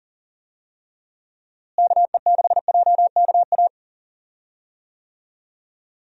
Morse code, heard as a keyed tone, about 40 wpm, 700 Hz. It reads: KE6JKA
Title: USA Call Signs - 6 Characters - Repeated 2x - 40wpm